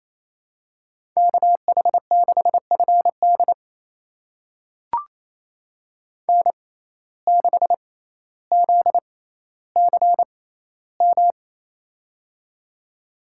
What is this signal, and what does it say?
Morse code, heard as a keyed tone, about 28 wpm, spaced as for 14 wpm, 700 Hz. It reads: KH6FD N6ZCM
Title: USA Call Signs - 5 Characters - 6x Character Spacing - 28wpm